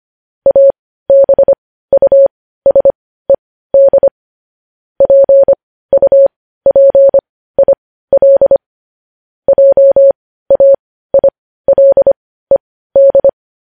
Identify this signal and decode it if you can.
Morse code, heard as a keyed tone, about 25 wpm, spaced as for 15 wpm, 550 Hz. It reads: ABUSED PUPIL JAILED
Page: news-headlines-in-morse-code-at-15-wpm